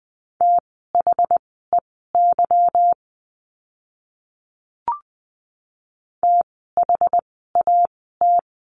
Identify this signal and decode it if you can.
Morse code, heard as a keyed tone, about 20 wpm, spaced as for 15 wpm, 700 Hz. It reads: THEY THAT
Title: 244 Lesson 39f - Letter z - Words 20wpm